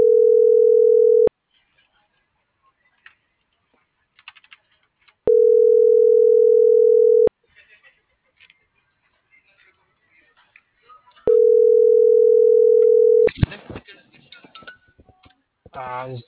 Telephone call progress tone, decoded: ringback tone

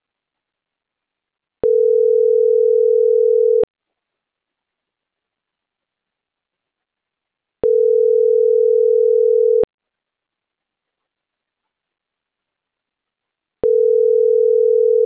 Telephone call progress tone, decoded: ringback tone

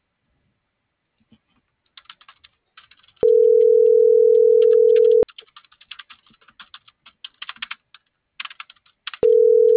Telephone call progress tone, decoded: ringback tone

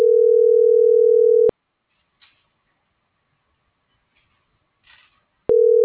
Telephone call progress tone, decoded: ringback tone

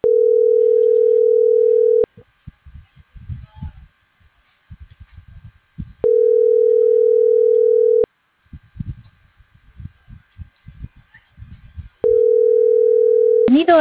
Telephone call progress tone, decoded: ringback tone

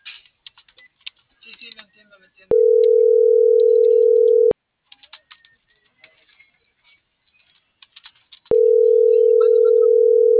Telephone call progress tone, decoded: ringback tone

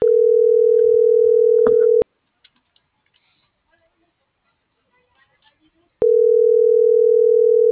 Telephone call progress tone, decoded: ringback tone

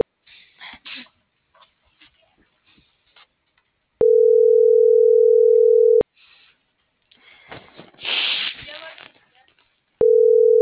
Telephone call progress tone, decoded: ringback tone

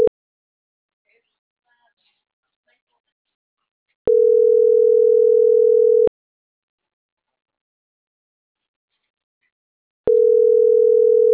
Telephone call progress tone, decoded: ringback tone

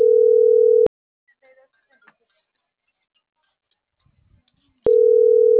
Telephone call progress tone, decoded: ringback tone

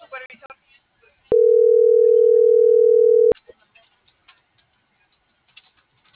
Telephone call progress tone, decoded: ringback tone

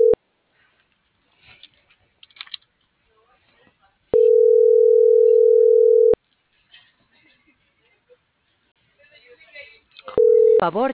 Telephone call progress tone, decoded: ringback tone

